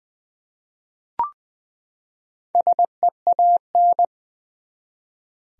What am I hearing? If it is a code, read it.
Morse code, heard as a keyed tone, about 20 wpm, 700 Hz. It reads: SEAN
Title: USA Common Names - 20wpm